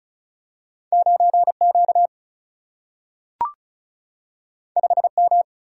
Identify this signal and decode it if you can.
Morse code, heard as a keyed tone, about 36 wpm, 700 Hz. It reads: 9 Q 5 M
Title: Single-Letter Single-Number 35wpm